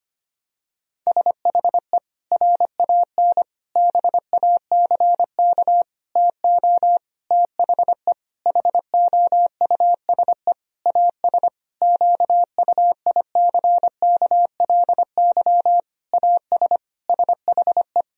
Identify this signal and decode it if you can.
Morse code, heard as a keyed tone, about 25 wpm, 700 Hz. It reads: SHE RAN BACK TO THE HOUSE AS QUICKLY AS SHE
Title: Sentences from Top 1000 Words - Review Part 3 - 25wpm